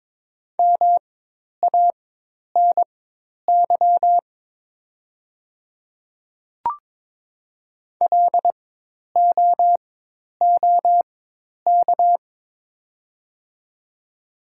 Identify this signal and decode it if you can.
Morse code, heard as a keyed tone, about 22 wpm, spaced as for 10 wpm, 700 Hz. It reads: MANY LOOK
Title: 4x Character Spacing - Top 100 Words - 22wpm